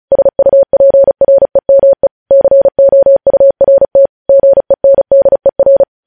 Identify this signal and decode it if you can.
Morse code, heard as a keyed tone, about 36 wpm, 550 Hz. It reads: SUPREME COURT GENDER
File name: podcast / News headlines-1x-35x35